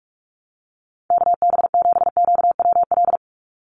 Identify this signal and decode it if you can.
Morse code, heard as a keyed tone, about 46 wpm, 690 Hz. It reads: KB7QWL
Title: USA Call Signs 45wpm